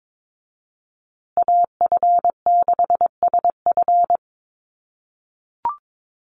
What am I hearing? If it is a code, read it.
Morse code, heard as a keyed tone, about 22 wpm, 700 Hz. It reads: AF6SF
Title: USA Call Signs Repeated 3x 22wpm